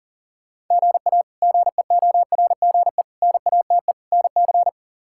Morse code, heard as a keyed tone, about 37 wpm, 700 Hz. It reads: GA GEORGE NATE NC
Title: K1USN SST QSO Practice - 40wpm